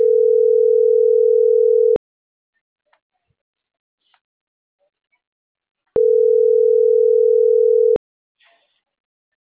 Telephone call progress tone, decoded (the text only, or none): ringback tone